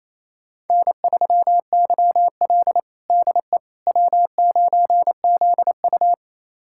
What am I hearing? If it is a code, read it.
Morse code, heard as a keyed tone, about 28 wpm, 700 Hz. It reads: N3YL DE W9ZU
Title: USA Callsign DE Callsign - 4 Characters - 28wpm